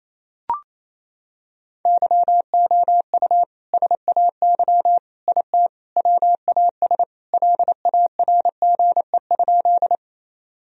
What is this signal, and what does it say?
Morse code, heard as a keyed tone, about 28 wpm, 700 Hz. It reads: YOU SAY IT WAS LARGE?
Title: Sentences from Top 1000 Words - Review Part 1 - 28wpm